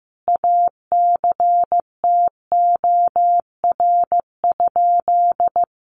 Morse code, heard as a keyed tone, about 15 wpm, 700 Hz. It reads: ACTOR?
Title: Sentences from Top 1500 Words 15wpm